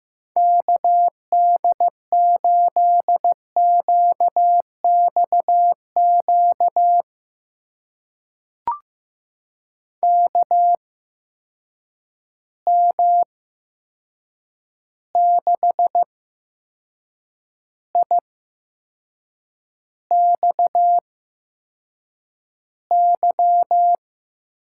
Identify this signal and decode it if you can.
Morse code, heard as a keyed tone, about 15 wpm, spaced as for 6 wpm, 700 Hz. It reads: KD8QXQ KM6IXY
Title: USA Call Signs - 6 Characters - 8x Character Spacing - 15wpm